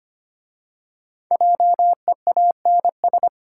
Morse code, heard as a keyed tone, about 26 wpm, 700 Hz. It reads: JEANS
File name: practice / 25wpm / Easy Words with Hints - 25wpm